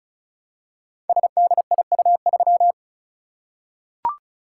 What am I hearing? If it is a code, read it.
Morse code, heard as a keyed tone, about 36 wpm, 700 Hz. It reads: SDIU3
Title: Code Groups - 5 Letter:Numbers - 35wpm